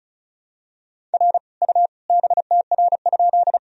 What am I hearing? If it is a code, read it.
Morse code, heard as a keyed tone, about 34 wpm, 700 Hz. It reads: R U BTR?
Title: Sentences from Top 400 Words and Abbreviations 35wpm